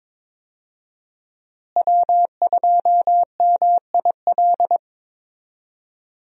Morse code, heard as a keyed tone, about 22 wpm, 700 Hz. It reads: W2MIL
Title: USA Call Signs - 5 Characters - Repeated 2x - 22wpm